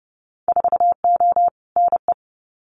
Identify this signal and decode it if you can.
Morse code, heard as a keyed tone, about 30 wpm, 700 Hz. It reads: 4O NE
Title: Winter Field Day Exchanges Encore 30wpm